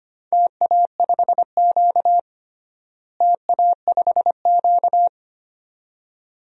Morse code, heard as a keyed tone, about 25 wpm, 700 Hz. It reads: TA5Q TA5Q
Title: DXCC-100 Call Signs - 4 Characters - Repeated 2x - 25wpm